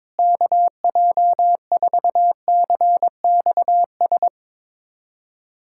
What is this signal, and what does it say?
Morse code, heard as a keyed tone, about 22 wpm, 700 Hz. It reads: KJ4CXS
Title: USA Callsign DE Callsign - 6 Characters - 22wpm